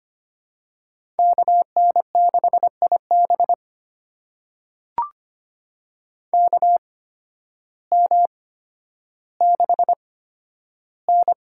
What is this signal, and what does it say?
Morse code, heard as a keyed tone, about 25 wpm, spaced as for 12 wpm, 700 Hz. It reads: KN6IB KM6N
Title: USA Call Signs - 8x Character Spacing - 25wpm